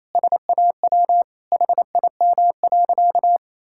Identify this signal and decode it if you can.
Morse code, heard as a keyed tone, about 28 wpm, 700 Hz. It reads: SAW HIM.
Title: Sentences from Top 1600 Words 28wpm